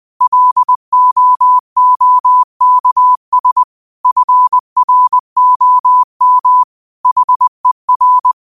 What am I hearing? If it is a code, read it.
Morse code, heard as a keyed tone, about 20 wpm, 1000 Hz. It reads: LOOKS FROM HER
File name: qotdfiles / 20wpm